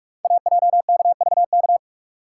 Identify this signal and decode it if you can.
Morse code, heard as a keyed tone, about 44 wpm, 690 Hz. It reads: AJKVK
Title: Code Groups - 5 Letters - 45wpm